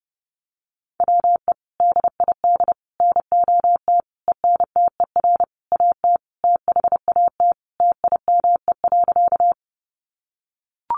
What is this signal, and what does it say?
Morse code, heard as a keyed tone, about 30 wpm, 700 Hz. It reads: WE DID NOT ENTER AT THAT TIME.
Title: Sentences from Top 1300 Words 30wpm